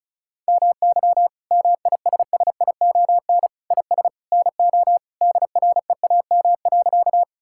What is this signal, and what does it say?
Morse code, heard as a keyed tone, about 34 wpm, 700 Hz. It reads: MY MISSION IS NO DREAM.
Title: Sentences from Top 1600 Words 35wpm